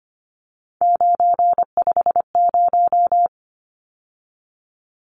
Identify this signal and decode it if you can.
Morse code, heard as a keyed tone, about 25 wpm, 700 Hz. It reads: 950
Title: Numbers - 3-Digit 25wpm